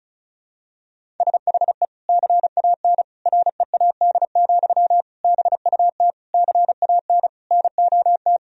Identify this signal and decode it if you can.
Morse code, heard as a keyed tone, about 34 wpm, 700 Hz. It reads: SHE CAN READ, BUT CAN NOT
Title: Sentences from Top 1000 Words - Review Part 1 - 35wpm